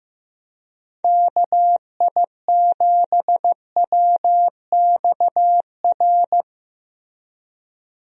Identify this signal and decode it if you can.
Morse code, heard as a keyed tone, about 15 wpm, 700 Hz. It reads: KI7WXR
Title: USA Call Signs - 6 Characters - Repeated 3x - 15wpm